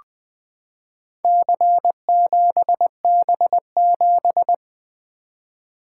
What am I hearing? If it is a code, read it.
Morse code, heard as a keyed tone, about 20 wpm, 700 Hz. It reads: C7B7
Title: Four Letter-Number Combinations - 20wpm